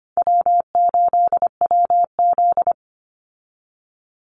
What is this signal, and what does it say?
Morse code, heard as a keyed tone, about 25 wpm, 700 Hz. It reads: W8WZ
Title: Easy Call Signs Repeated 3x - 25wpm